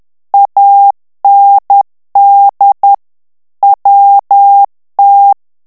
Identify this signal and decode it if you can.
Morse code, heard as a keyed tone, about 10 wpm, 800 Hz. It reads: AND WT
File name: ANGLAIS texte clair 3 vitesse 10 mots minute